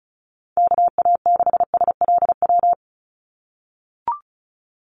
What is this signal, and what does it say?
Morse code, heard as a keyed tone, about 34 wpm, 700 Hz. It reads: KA6SLW
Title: USA Call Signs 35wpm